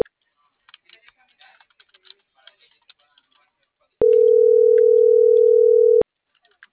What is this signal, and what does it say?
Ringback tone (North America), 440 + 480 Hz.